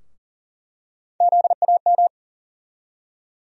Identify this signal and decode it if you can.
Morse code, heard as a keyed tone, about 40 wpm, 700 Hz. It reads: ZAM
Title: USA Call Sign Postfixes - 3 Letters 40wpm